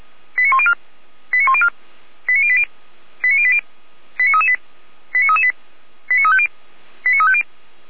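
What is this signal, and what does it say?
Five-tone selective call (ZVEI), heard as quick runs of five tones, tones of about 80 ms. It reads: 89104 89104 89080 89080 89338 89338 89350 89350
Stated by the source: ZVEI five-tone selective call received with an SDR, from the audio server of the Orlamünde fire brigade (feuerwehr-orlamuende).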